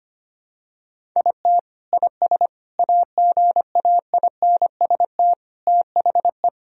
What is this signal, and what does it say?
Morse code, heard as a keyed tone, about 25 wpm, 700 Hz. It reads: IT IS AGAINST THE
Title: Sentences from Top 1000 Words - Review Part 3 - 25wpm